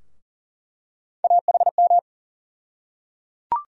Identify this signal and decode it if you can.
Morse code, heard as a keyed tone, about 38 wpm, 700 Hz. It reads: AHM
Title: USA Call Sign Postfixes - 3 Letters 40wpm